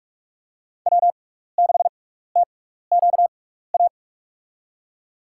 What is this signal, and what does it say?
Morse code, heard as a keyed tone, about 42 wpm, spaced as for 15 wpm, 690 Hz. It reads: W6TQA
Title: USA Call Signs - 5 Characters - 6x Character Spacing - 45wpm